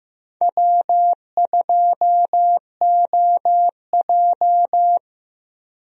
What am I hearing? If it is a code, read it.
Morse code, heard as a keyed tone, about 15 wpm, 700 Hz. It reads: W2OJ
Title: USA Call Signs - 4 Characters - Repeated 3x - 15wpm